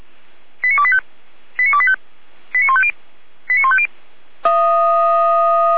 Five-tone selective call (ZVEI), heard as quick runs of five tones, tones of about 70 ms. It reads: 89285 89285 89150 89150